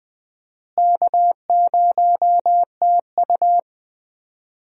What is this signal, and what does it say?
Morse code, heard as a keyed tone, about 20 wpm, 700 Hz. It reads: K0TU
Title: USA Call Signs - 4 Characters - Repeated 2x - 20wpm